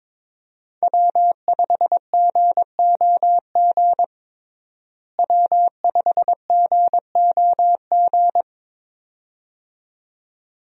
Morse code, heard as a keyed tone, about 22 wpm, 700 Hz. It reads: W5GOG W5GOG
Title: USA Call Signs - 5 Characters - Repeated 2x - 22wpm